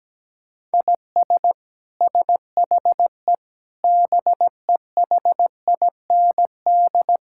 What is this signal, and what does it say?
Morse code, heard as a keyed tone, about 17 wpm, 700 Hz. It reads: IS SHE BEHIND
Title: Sentences from Top 1000 Words - Review Part 6 - 17wpm